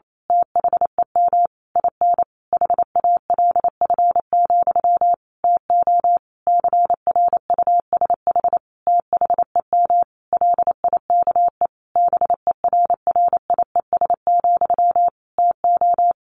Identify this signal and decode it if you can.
Morse code, heard as a keyed tone, about 28 wpm, 700 Hz. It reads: THEM IN HALF, TO CRUSH THEM LIKE BERRIES, TO
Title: A Sound Of Thunder 28wpm